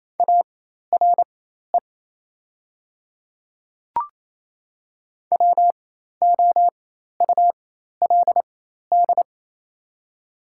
Morse code, heard as a keyed tone, about 28 wpm, spaced as for 13 wpm, 700 Hz. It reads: ARE WOULD
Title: 4x Character Spacing - Top 100 Words - 28wpm